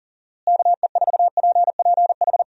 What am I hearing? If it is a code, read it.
Morse code, heard as a keyed tone, about 40 wpm, 700 Hz. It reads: KE4PPH
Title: USA Call Signs - 6 Characters - Repeated 3x - 40wpm